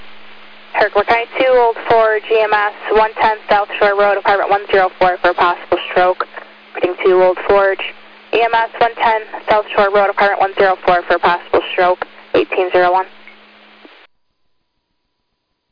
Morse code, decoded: none

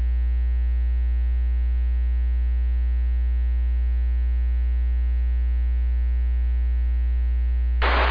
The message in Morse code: none

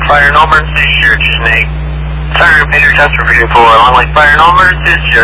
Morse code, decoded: none

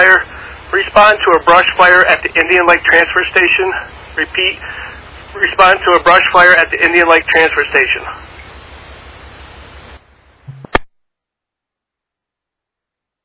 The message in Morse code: none

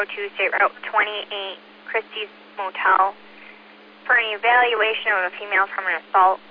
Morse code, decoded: none

